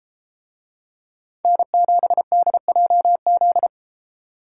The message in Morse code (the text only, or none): N7DJZ